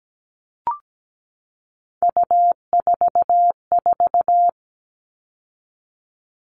Morse code, heard as a keyed tone, about 17 wpm, 700 Hz. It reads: U44
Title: All Cut Numbers - 3-Digit 17wpm